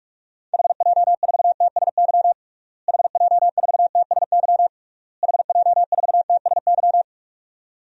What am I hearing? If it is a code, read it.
Morse code, heard as a keyed tone, about 46 wpm, 700 Hz. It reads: HJ4TSY HJ4TSY HJ4TSY
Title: DXCC-100 Call Signs - 6 Characters - Repeated 3x - 45wpm